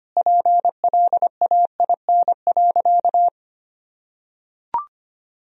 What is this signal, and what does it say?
Morse code, heard as a keyed tone, about 25 wpm, 700 Hz. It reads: PLAIN.
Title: Sentences from Top 1000 Words - Review Part 2 - 25wpm